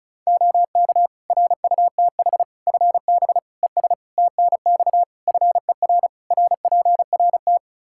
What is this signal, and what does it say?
Morse code, heard as a keyed tone, about 36 wpm, 700 Hz. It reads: OK RUTH FB ES TNX FER RPRT
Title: Practice QSO - Exchange 2 - Info - 35wpm